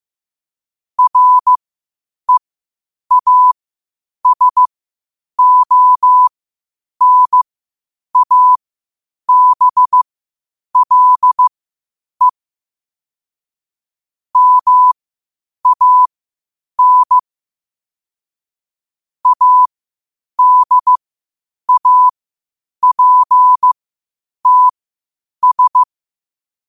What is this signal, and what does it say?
Morse code, heard as a keyed tone, about 15 wpm, spaced as for 8 wpm, 1000 Hz. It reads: REASONABLE MAN ADAPTS
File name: qotdfiles / 8wpm